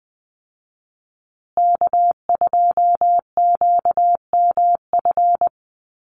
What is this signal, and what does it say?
Morse code, heard as a keyed tone, about 20 wpm, 700 Hz. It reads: K2QMF
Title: Call Signs Repeated 2x - 20wpm